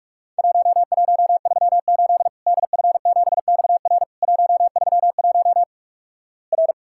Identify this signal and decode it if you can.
Morse code, heard as a keyed tone, about 47 wpm, 700 Hz. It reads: 1138 DF7XR 131 R
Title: Worked All Europe DX Contest - QTC Exchange - 45wpm